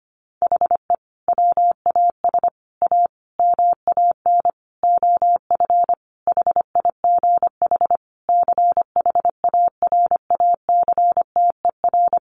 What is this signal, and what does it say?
Morse code, heard as a keyed tone, about 25 wpm, 700 Hz. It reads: HE WAS A MAN OF HIGH CHARACTER
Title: Sentences from Top 1200 Words 25wpm